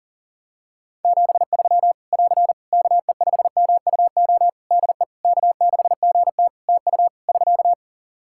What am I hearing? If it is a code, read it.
Morse code, heard as a keyed tone, about 40 wpm, 700 Hz. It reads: 73 <AR> KE5MUO DE K6GT TU <SK>
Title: Practice QSO - Exchange 4 - Ending - 40wpm